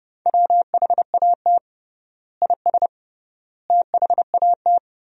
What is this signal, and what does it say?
Morse code, heard as a keyed tone, about 32 wpm, 700 Hz. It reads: WHAT IS THAT
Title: 3x Word-Spacing - Sentences from Top 200 Words - 30wpm